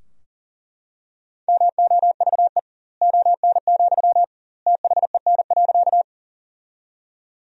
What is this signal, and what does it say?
Morse code, heard as a keyed tone, about 40 wpm, 700 Hz. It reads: MOVE ON, THEN.
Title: 2x Word-Spacing - Sentences from Top 200 Words - 40wpm